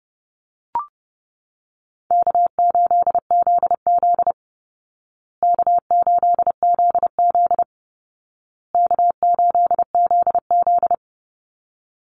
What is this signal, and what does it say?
Morse code, heard as a keyed tone, about 30 wpm, 700 Hz. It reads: K8ZZ K8ZZ K8ZZ
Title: Easy Call Signs Repeated 3x - 30wpm